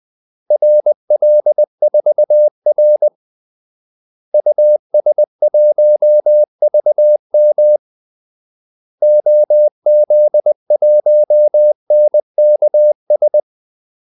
Morse code, heard as a keyed tone, about 20 wpm, 600 Hz. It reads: RL4R US1VM OZ1NKS
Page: Kaldesignaler 20 wpm